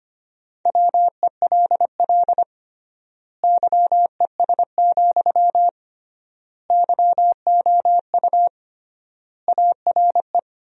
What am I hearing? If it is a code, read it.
Morse code, heard as a keyed tone, about 25 wpm, 700 Hz. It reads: WELL YES, YOU ARE
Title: 3x Word-Spacing - Sentences from Top 500 Words - 25wpm